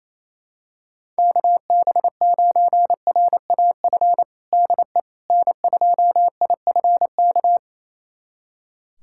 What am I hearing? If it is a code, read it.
Morse code, heard as a keyed tone, about 28 wpm, 700 Hz. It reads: KB9RAF DE N2IFK
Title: USA Callsign DE Callsign 28wpm